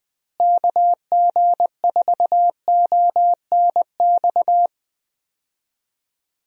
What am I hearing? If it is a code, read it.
Morse code, heard as a keyed tone, about 20 wpm, 700 Hz. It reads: KG4ONX
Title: USA Call Signs - 6 Characters - Repeated 2x - 20wpm